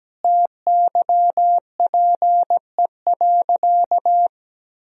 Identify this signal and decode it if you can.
Morse code, heard as a keyed tone, about 17 wpm, 700 Hz. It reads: TYPE.
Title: Sentences from Top 1000 Words - Review Part 3 - 17wpm